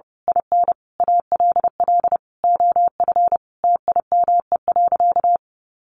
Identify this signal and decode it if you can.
Morse code, heard as a keyed tone, about 30 wpm, 700 Hz. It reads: IN ALL OF TIME.
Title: A Sound Of Thunder 30wpm